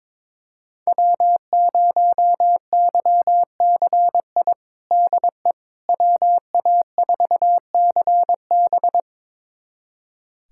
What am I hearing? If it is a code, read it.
Morse code, heard as a keyed tone, about 22 wpm, 700 Hz. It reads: W0YCI DE WA4CB
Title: USA Callsign DE Callsign - 5 Characters - 22wpm